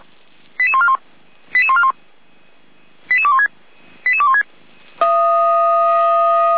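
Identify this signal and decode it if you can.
Five-tone selective call (ZVEI), heard as quick runs of five tones, tones of about 70 ms. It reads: 88141 88141 88216 88216
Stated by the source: ZVEI five-tone call 88217 received on an SDR.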